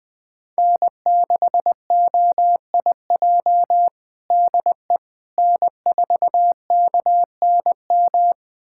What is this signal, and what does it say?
Morse code, heard as a keyed tone, about 20 wpm, 700 Hz. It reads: N6OIJ DE N4KNM
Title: USA Callsign DE Callsign - 5 Characters - 20wpm